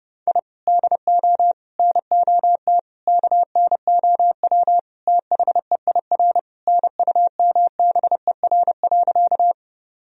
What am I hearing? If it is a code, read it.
Morse code, heard as a keyed tone, about 30 wpm, 700 Hz. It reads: I DO NOT KNOW THEIR NUMBER.
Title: Sentences from Top 1000 Words - Review Part 1 - 30wpm